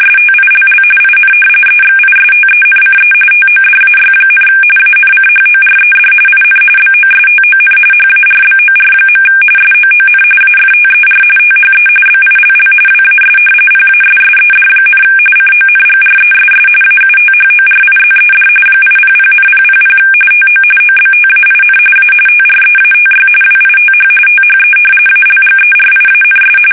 RTTY, about 45.45 baud. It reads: STILL IN EXISTENCE TODAY, ANCHORED IN BOSTON HARBOR. SOON AFTER THE BILL WAS PASSED, CONGRESS AUTHORIZED 800,000 TO OBTAIN A TREATY WITH THE ALGERIANS
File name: fsk850 / usnavy1